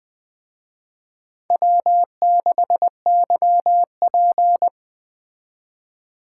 Morse code, heard as a keyed tone, about 20 wpm, 700 Hz. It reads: W6YP